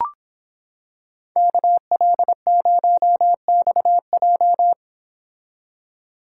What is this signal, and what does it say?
Morse code, heard as a keyed tone, about 26 wpm, 700 Hz. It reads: KL0XJ